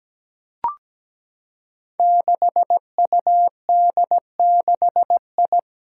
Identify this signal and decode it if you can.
Morse code, heard as a keyed tone, about 17 wpm, 700 Hz. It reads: 6UD6I
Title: Five Challenging Letter-Number Combinations 17wpm